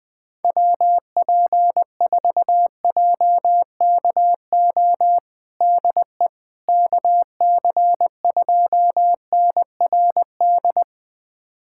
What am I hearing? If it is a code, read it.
Morse code, heard as a keyed tone, about 21 wpm, 700 Hz. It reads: WP4JKO DE KC2NRD
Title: USA Callsign DE Callsign 20wpm